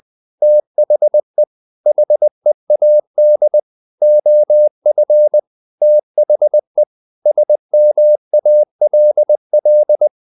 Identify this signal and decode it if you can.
Morse code, heard as a keyed tone, about 20 wpm, 600 Hz. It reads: THE HEAD OF THE SMALL
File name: the old man and the sea / the old man and the sea part 1 20wpm